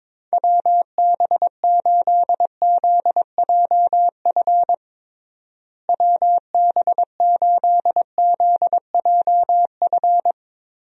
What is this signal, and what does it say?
Morse code, heard as a keyed tone, about 22 wpm, 700 Hz. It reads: WB8ZJF WB8ZJF
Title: USA Call Signs - 6 Characters - Repeated 3x - 22wpm